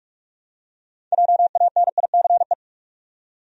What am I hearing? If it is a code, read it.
Morse code, heard as a keyed tone, about 46 wpm, 700 Hz. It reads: JANICE
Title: USA Common Names - 45wpm